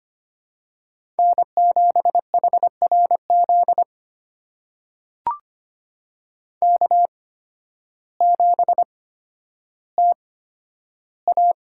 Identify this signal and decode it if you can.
Morse code, heard as a keyed tone, about 25 wpm, spaced as for 12 wpm, 700 Hz. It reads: N7HRZ K7TA
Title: USA Call Signs - 5 Characters - 8x Character Spacing - 25wpm